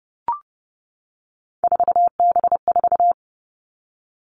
Morse code, heard as a keyed tone, about 30 wpm, 700 Hz. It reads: 4B4
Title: Three Challenging Letter-Number Combinations 30wpm